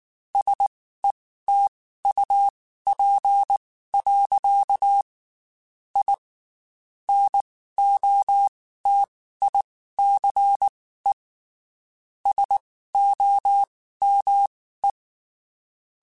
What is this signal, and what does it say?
Morse code, heard as a keyed tone, about 19 wpm, spaced as for 14 wpm, 800 Hz. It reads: SETUP. I NOTICE SOME
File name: QSOs / QSOs-mp3s / WD0BGP 1@18